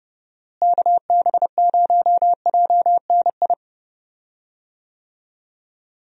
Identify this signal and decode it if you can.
Morse code, heard as a keyed tone, about 30 wpm, 700 Hz. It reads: KB0JNI
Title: USA Call Signs - 6 Characters - Repeated 2x - 30wpm